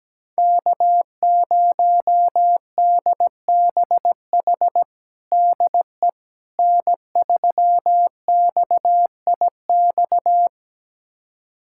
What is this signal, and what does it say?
Morse code, heard as a keyed tone, about 17 wpm, 700 Hz. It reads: K0DBH DE N3XIX
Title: USA Callsign DE Callsign - 5 Characters - 17wpm